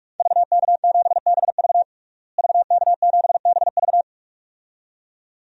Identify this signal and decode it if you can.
Morse code, heard as a keyed tone, about 41 wpm, 700 Hz. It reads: VK7BV VK7BV
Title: DXCC-100 Call Signs - 5 Characters - Repeated 2x - 45wpm